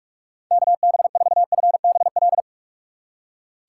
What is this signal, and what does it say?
Morse code, heard as a keyed tone, about 47 wpm, 700 Hz. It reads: KB4FBL
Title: USA Call Signs - 6 Characters - 4x Character Spacing - 45wpm